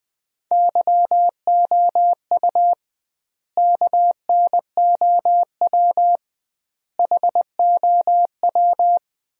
Morse code, heard as a keyed tone, about 20 wpm, 700 Hz. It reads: YOU KNOW HOW
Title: 2x Word-Spacing - Sentences from Top 100 Words - 20wpm